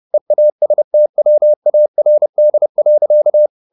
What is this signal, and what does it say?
Morse code, heard as a keyed tone, about 30 wpm, 600 Hz. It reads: EASTWARD.
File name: the old man and the sea / the old man and the sea 1 30wpm